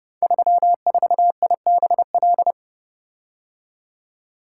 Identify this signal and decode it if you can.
Morse code, heard as a keyed tone, about 30 wpm, 700 Hz. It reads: 34IBL
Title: Code Groups - 5 Letter:Numbers - 30wpm